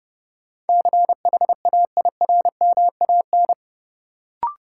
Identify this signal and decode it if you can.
Morse code, heard as a keyed tone, about 32 wpm, 700 Hz. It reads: CHAIRMAN
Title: Sets of 3 Words - New Word Drill - 30wpm